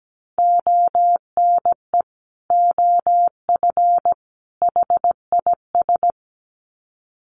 Morse code, heard as a keyed tone, about 17 wpm, 700 Hz. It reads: ONE OF HIS
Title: Sets of 3 Words 17wpm